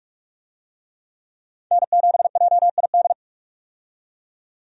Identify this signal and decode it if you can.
Morse code, heard as a keyed tone, about 45 wpm, 700 Hz. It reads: N7JID
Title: USA Call Signs - 5 Characters - 45wpm